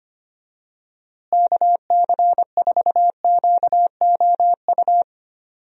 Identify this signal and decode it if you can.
Morse code, heard as a keyed tone, about 25 wpm, 700 Hz. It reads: KC4QOU
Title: USA Call Signs - 6 Characters - Repeated 2x - 25wpm